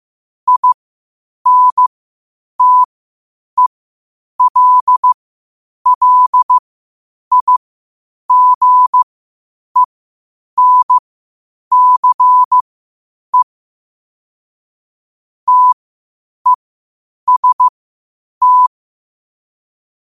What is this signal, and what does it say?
Morse code, heard as a keyed tone, about 15 wpm, spaced as for 8 wpm, 1000 Hz. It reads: INTELLIGENCE TEST
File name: qotdfiles / 8wpm